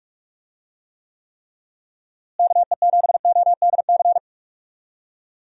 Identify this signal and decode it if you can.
Morse code, heard as a keyed tone, about 46 wpm, 700 Hz. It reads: KE7ODC
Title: USA Call Signs - 6 Characters - Repeated 3x - 45wpm